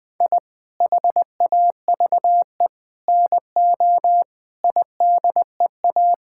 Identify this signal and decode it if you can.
Morse code, heard as a keyed tone, about 20 wpm, 700 Hz. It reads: I HAVE NO IDEA
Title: Sentences with 9-Word Phrases - 20wpm